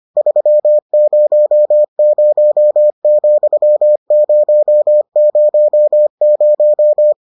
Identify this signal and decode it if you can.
Morse code, heard as a keyed tone, about 25 wpm, 600 Hz. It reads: 300,000